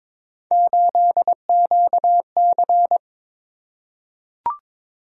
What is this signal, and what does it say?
Morse code, heard as a keyed tone, about 23 wpm, 700 Hz. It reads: 8QC
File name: practice / 22wpm / Three Letter-Number Combinations - 22wpm